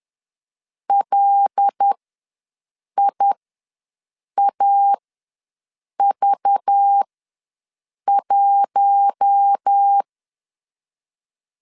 Morse code, heard as a keyed tone, about 10 wpm, spaced as for 6 wpm, 800 Hz. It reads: LIAV1
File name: LEZIONI / 6 PKX0 / 6 PKX ws 10wpm 3706